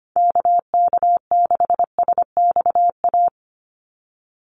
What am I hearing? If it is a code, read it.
Morse code, heard as a keyed tone, about 25 wpm, 700 Hz. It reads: KK6SXA
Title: USA Call Signs 25wpm